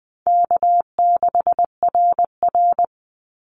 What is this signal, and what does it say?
Morse code, heard as a keyed tone, about 20 wpm, 700 Hz. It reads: K6RR
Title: Easy Call Signs Repeated 2x - 20wpm